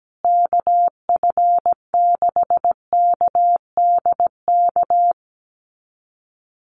Morse code, heard as a keyed tone, about 17 wpm, 700 Hz. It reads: KF6KDK
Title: USA Call Signs Repeated 3x 17wpm